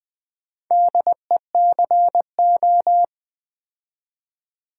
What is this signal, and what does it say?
Morse code, heard as a keyed tone, about 20 wpm, 700 Hz. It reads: DECO
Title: Build A Word - 2 Letters At A Time - 8-Letter Words - 20wpm